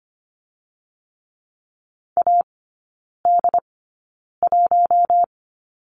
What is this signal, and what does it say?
Morse code, heard as a keyed tone, about 25 wpm, spaced as for 9 wpm, 700 Hz. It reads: AD1
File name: Lesson 15e - Number 5 - Three Characters 25wpm